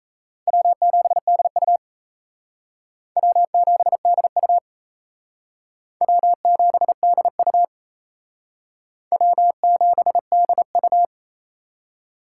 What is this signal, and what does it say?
Morse code, heard as a keyed tone, about 35 wpm, 700 Hz. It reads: W7DU W7DU W7DU W7DU